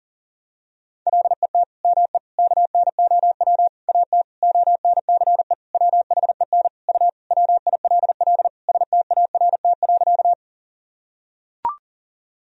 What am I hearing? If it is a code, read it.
Morse code, heard as a keyed tone, about 40 wpm, 700 Hz. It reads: LET ME KNOW AT ONCE WHEN U WILL START.
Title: Sentences from Top 300 Words and Abbreviations 40wpm